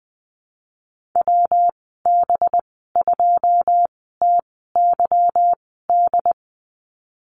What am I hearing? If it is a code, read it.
Morse code, heard as a keyed tone, about 20 wpm, spaced as for 15 wpm, 700 Hz. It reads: WB2TYD
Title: 193 Lesson 31g - Letter v - Call Signs 20wpm